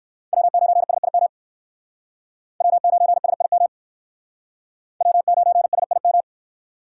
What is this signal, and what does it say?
Morse code, heard as a keyed tone, about 56 wpm, 700 Hz. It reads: W9SIM W9SIM W9SIM
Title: Speed Racing - USA 1x3 Call Signs 45wpm